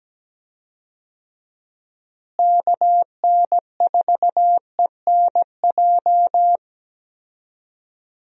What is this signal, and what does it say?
Morse code, heard as a keyed tone, about 17 wpm, 700 Hz. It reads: KN4ENJ
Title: USA Call Signs - 6 Characters - Repeated 2x - 17wpm